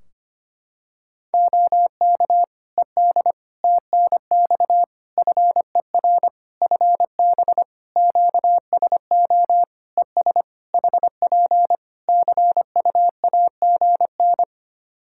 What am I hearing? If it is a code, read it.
Morse code, heard as a keyed tone, about 25 wpm, 700 Hz. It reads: OK ED TNX FER FB QSO ES HP CUAGN
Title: K1USN SST QSO Practice 25wpm